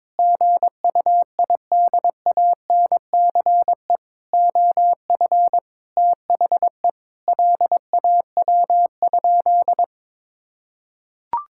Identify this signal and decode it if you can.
Morse code, heard as a keyed tone, about 22 wpm, 700 Hz. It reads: GUIDANCE OF THE LAW?
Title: Sentences from Top 1600 Words 22wpm